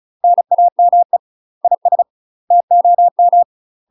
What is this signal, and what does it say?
Morse code, heard as a keyed tone, about 33 wpm, 700 Hz. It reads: NAME IS TOM
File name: UK MASTER QSOs TRAINING / 35WPM / UK MASTER QSOs 0012